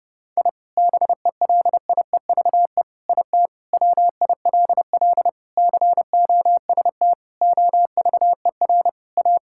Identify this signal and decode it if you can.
Morse code, heard as a keyed tone, about 30 wpm, 700 Hz. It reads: I BELIEVE IT WILL COST OVER A
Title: Sentences from Top 1000 Words - Review Part 5 - 30wpm